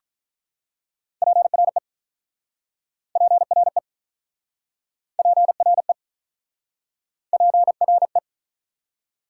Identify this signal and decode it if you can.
Morse code, heard as a keyed tone, about 48 wpm, 700 Hz. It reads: PRE PRE PRE PRE